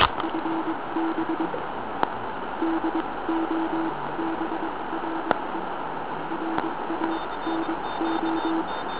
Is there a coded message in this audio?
Yes: Morse code from 0.2 to 8.6 s, heard as a keyed tone, about 22 wpm, spaced as for 18 wpm, 330 Hz, with noise. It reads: FB DOBRE RANO